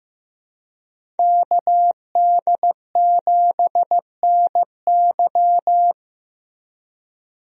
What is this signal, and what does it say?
Morse code, heard as a keyed tone, about 15 wpm, 700 Hz. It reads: KD7NY
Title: USA Call Signs - 5 Characters - Repeated 3x - 15wpm